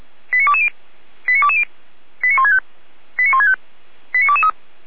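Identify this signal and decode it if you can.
Five-tone selective call (ZVEI), heard as quick runs of five tones, tones of about 70 ms. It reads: 89220 89220 89165 89165 89223
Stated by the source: ZVEI five-tone call received on an SDR.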